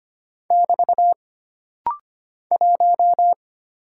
Morse code, heard as a keyed tone, about 25 wpm, 700 Hz. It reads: <BT> 1
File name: Single Character (Letter:Number:Punctuation:Prosign) - Rapid-Fire - 25wpm